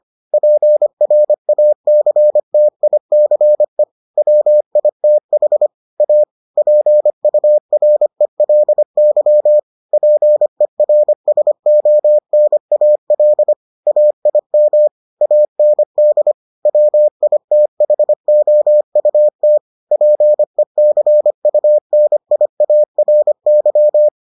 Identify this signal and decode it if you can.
Morse code, heard as a keyed tone, about 25 wpm, 600 Hz. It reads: PRACTICE WITH A PURELY PERSONAL AIM AND WITHOUT PECUNIARY